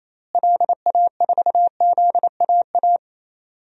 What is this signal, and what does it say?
Morse code, heard as a keyed tone, about 28 wpm, 700 Hz. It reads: LA4ZAA
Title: DXCC-100 Call Signs - 6 Characters - Repeated 2x - 28wpm